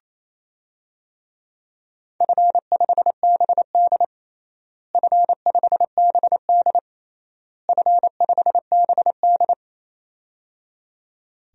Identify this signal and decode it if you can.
Morse code, heard as a keyed tone, about 28 wpm, 700 Hz. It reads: F5BD F5BD F5BD
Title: DXCC-100 Call Signs - 4 Characters - Repeated 3x - 28wpm